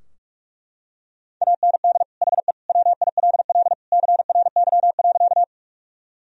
Morse code, heard as a keyed tone, about 41 wpm, 700 Hz. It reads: AND HE WILL CRY.
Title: Sentences from Top 1000 Words - Review Part 2 - 45wpm